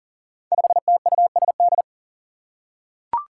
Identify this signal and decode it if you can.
Morse code, heard as a keyed tone, about 40 wpm, 700 Hz. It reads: 5TUSD
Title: Five Challenging Letter-Number Combinations 40wpm